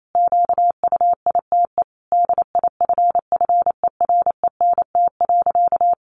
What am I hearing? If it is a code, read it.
Morse code, heard as a keyed tone, about 28 wpm, 700 Hz. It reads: QUITE DIFFERENT.
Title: Sentences from Top 700 Words 28wpm